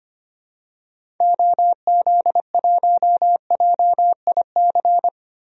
Morse code, heard as a keyed tone, about 25 wpm, 700 Hz. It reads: OZ1JIC
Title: DXCC-100 Call Signs - 6 Characters - 25wpm